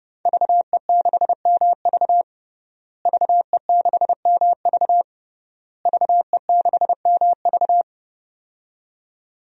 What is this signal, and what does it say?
Morse code, heard as a keyed tone, about 30 wpm, 700 Hz. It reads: VE6MV VE6MV VE6MV
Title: DXCC-100 Call Signs - 5 Characters - Repeated 3x - 30wpm